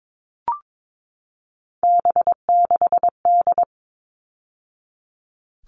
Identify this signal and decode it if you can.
Morse code, heard as a keyed tone, about 22 wpm, 700 Hz. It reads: B6D
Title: All Cut Numbers - 3-Digit 22wpm